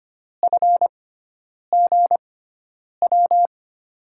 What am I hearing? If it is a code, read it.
Morse code, heard as a keyed tone, about 25 wpm, spaced as for 9 wpm, 700 Hz. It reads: FGW
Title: USA Call Signs - 6 Characters - 6x Character Spacing - 25wpm